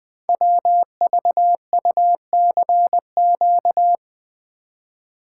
Morse code, heard as a keyed tone, about 20 wpm, 700 Hz. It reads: WVUCQ
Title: Code Groups - 5 Letters - 20wpm